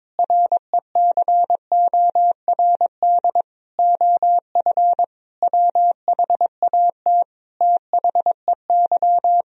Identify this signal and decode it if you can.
Morse code, heard as a keyed tone, about 21 wpm, 700 Hz. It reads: RECORD OF WHAT THEY
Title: Sentences from Top 1000 Words - Review Part 3 - 22wpm